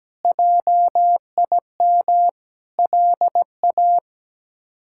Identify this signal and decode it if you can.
Morse code, heard as a keyed tone, about 17 wpm, 700 Hz. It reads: JIM LA
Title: K1USN SST QSO Practice - 17wpm